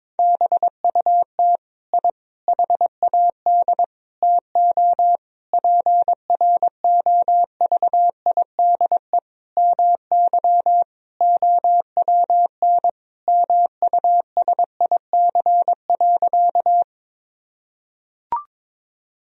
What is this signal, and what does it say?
Morse code, heard as a keyed tone, about 21 wpm, 700 Hz. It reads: BUT I HAD TO PROVIDE MY OWN MUSIC.
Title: Sentences from Top 1000 Words - Review Part 3 - 22wpm